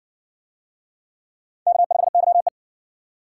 Morse code, heard as a keyed tone, about 60 wpm, 700 Hz. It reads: K5YE